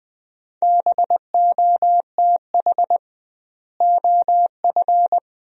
Morse code, heard as a keyed tone, about 20 wpm, 700 Hz. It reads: BOTH OF
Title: 2x Word-Spacing - Sentences from Top 300 Words - 20wpm